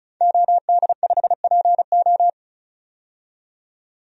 Morse code, heard as a keyed tone, about 34 wpm, 700 Hz. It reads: OD5PO
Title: DXCC-100 Call Signs - 5 Characters - 35wpm